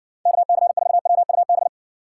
Morse code, heard as a keyed tone, about 61 wpm, 700 Hz. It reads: KC4WUB